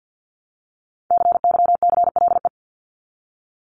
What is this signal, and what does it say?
Morse code, heard as a keyed tone, about 50 wpm, 700 Hz. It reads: CYCLE
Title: Top 751-1000 Words 50wpm